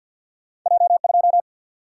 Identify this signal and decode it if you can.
Morse code, heard as a keyed tone, about 54 wpm, 700 Hz. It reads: J2
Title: Single-Letter Single-Number 50wpm